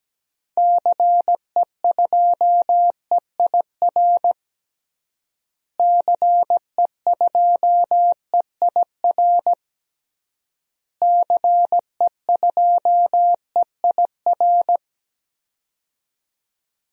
Morse code, heard as a keyed tone, about 17 wpm, 700 Hz. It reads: CE2EIR CE2EIR CE2EIR
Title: DXCC-100 Call Signs - 6 Characters - Repeated 3x - 17wpm